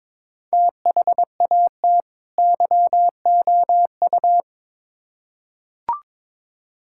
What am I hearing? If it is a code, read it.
Morse code, heard as a keyed tone, about 22 wpm, 700 Hz. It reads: THAT YOU
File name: Build A Phrase - 2-Word Phrases - 22wpm